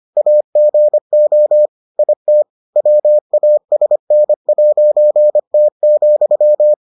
Morse code, heard as a keyed tone, about 25 wpm, 600 Hz. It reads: AGO IT WASN'T,